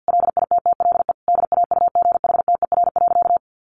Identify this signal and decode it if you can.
Morse code, heard as a keyed tone, about 48 wpm, 700 Hz. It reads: LITTLE DAUGHTER.
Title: Sentences from Top 1500 Words 50wpm